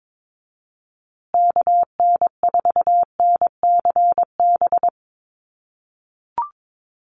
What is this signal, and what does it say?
Morse code, heard as a keyed tone, about 22 wpm, 700 Hz. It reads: KN4NCB
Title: USA Call Signs Repeated 3x 22wpm